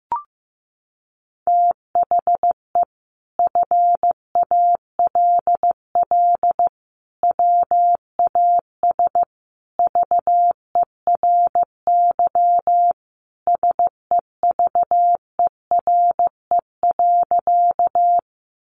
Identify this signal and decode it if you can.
Morse code, heard as a keyed tone, about 15 wpm, 700 Hz. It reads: THE FALL WAS VERY SEVERE.
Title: Sentences from Top 1400 Words 15wpm